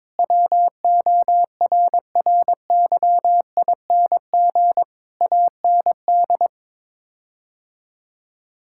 Morse code, heard as a keyed tone, about 22 wpm, 700 Hz. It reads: WORRYING AND
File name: practice / 22wpm / Sets of 8 Words - 22wpm